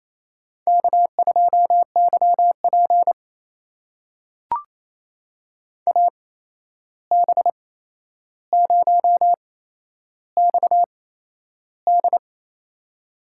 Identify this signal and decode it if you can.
Morse code, heard as a keyed tone, about 28 wpm, spaced as for 11 wpm, 700 Hz. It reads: K2YP AB0XD
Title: USA Call Signs - 8x Character Spacing - 28wpm